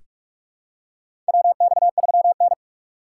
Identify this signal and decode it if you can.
Morse code, heard as a keyed tone, about 46 wpm, 700 Hz. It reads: WX3N
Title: USA Call Signs - 2x Character Spacing - 45wpm